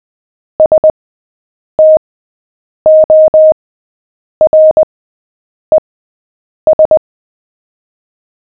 Morse code, heard as a keyed tone, about 20 wpm, spaced as for 8 wpm, 620 Hz. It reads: STORES
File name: cw / fists / Short-Story-7-8wpm